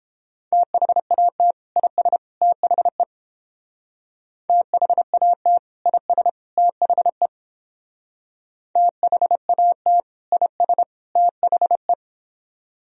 Morse code, heard as a keyed tone, about 29 wpm, 700 Hz. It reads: THAT IS THE THAT IS THE THAT IS THE